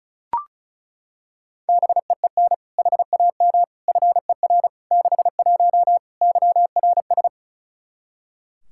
Morse code, heard as a keyed tone, about 34 wpm, 700 Hz. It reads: BEEN HAM FER 61 YRS
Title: Practice QSO - Exchange 3 - More Info - 35wpm